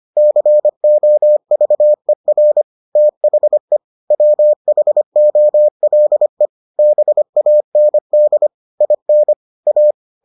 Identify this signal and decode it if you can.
Morse code, heard as a keyed tone, about 25 wpm, 600 Hz. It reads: COVER THE WHOLE BAND IN A